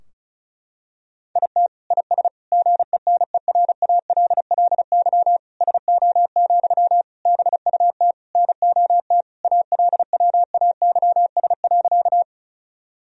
Morse code, heard as a keyed tone, about 36 wpm, 700 Hz. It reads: IT IS GENERALLY SO, BUT NOT ALWAYS.
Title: Sentences from Top 1000 Words - Review Part 4 - 35wpm